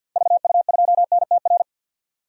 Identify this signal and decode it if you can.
Morse code, heard as a keyed tone, about 52 wpm, 700 Hz. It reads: VU2NTR
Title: DXCC-100 Call Signs - 6 Characters - 50wpm